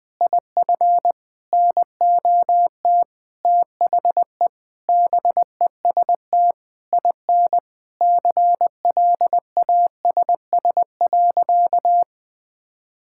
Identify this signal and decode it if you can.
Morse code, heard as a keyed tone, about 20 wpm, 700 Hz. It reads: IF NOT THE BEST IN CLASS.
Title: Sentences with 8-Word Phrases - 20wpm